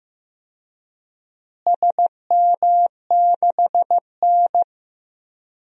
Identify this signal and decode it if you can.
Morse code, heard as a keyed tone, about 15 wpm, 700 Hz. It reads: SM6N
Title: DXCC-100 Call Signs - 4 Characters - Repeated 2x - 15wpm